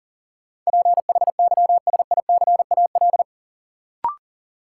Morse code, heard as a keyed tone, about 40 wpm, 700 Hz. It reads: PHYSICAL
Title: Sentences from Top 1000 Words - Review Part 4 - New Word Drill - 40wpm